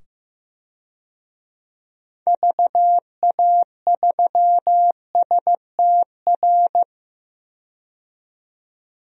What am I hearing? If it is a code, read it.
Morse code, heard as a keyed tone, about 15 wpm, 700 Hz. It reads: VA3STR
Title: DXCC-100 Call Signs - 6 Characters - 15wpm